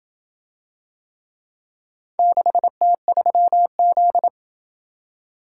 Morse code, heard as a keyed tone, about 26 wpm, 700 Hz. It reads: 6T3Z